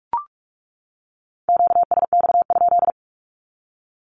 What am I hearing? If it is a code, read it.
Morse code, heard as a keyed tone, about 44 wpm, 700 Hz. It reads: QSX?
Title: Q-Codes 45wpm